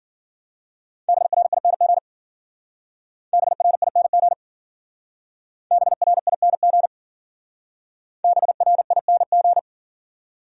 Morse code, heard as a keyed tone, about 50 wpm, 700 Hz. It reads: BRING BRING BRING BRING
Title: Speed-Racing Top 500 Words 40wpm